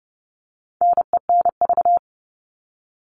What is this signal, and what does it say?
Morse code, heard as a keyed tone, about 30 wpm, 700 Hz. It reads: NENV
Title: All Cut Numbers - 4-Digit 30wpm